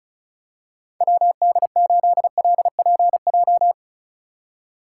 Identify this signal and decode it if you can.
Morse code, heard as a keyed tone, about 36 wpm, 700 Hz. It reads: WD8LPJ
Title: USA Call Signs - 6 Characters - 3x Character Spacing - 35wpm